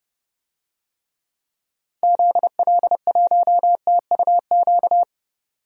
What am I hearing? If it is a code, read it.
Morse code, heard as a keyed tone, about 32 wpm, 700 Hz. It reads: ZL1TUQ
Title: DXCC-100 Call Signs - 6 Characters - 30wpm